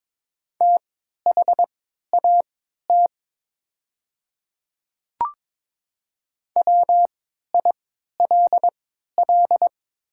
Morse code, heard as a keyed tone, about 22 wpm, spaced as for 13 wpm, 700 Hz. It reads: THAT WILL
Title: 3x Character Spacing - Top 100 Words - 22wpm